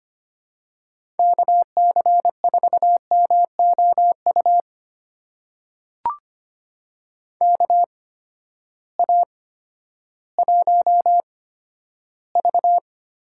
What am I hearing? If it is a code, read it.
Morse code, heard as a keyed tone, about 25 wpm, spaced as for 12 wpm, 700 Hz. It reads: KC4MOU KA1V
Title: USA Call Signs - 6 Characters - 8x Character Spacing - 25wpm